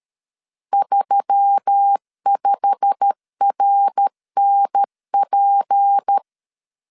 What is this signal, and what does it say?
Morse code, heard as a keyed tone, about 13 wpm, 800 Hz. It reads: 35RNP
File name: CW / DET / 1 9 1500C S A 12wpm 3800